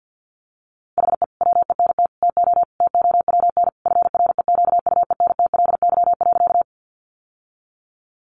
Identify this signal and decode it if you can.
Morse code, heard as a keyed tone, about 50 wpm, 700 Hz. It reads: HE WENT TO TOWN FREQUENTLY.
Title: Sentences from Top 1300 Words 50wpm